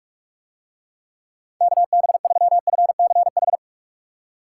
Morse code, heard as a keyed tone, about 47 wpm, 700 Hz. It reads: KB3FCH